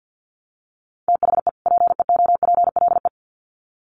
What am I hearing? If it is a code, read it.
Morse code, heard as a keyed tone, about 50 wpm, 700 Hz. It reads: THE PEOPLE
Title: Sets of 2 Words 50wpm